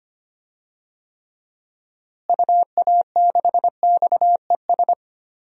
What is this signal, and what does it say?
Morse code, heard as a keyed tone, about 25 wpm, 700 Hz. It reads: UA6XES
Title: DXCC-100 Call Signs - 6 Characters - Repeated 2x - 25wpm